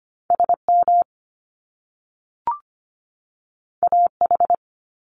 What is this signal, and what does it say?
Morse code, heard as a keyed tone, about 25 wpm, 700 Hz. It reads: S M A H
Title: Top Two Letter Combinations in Words 25wpm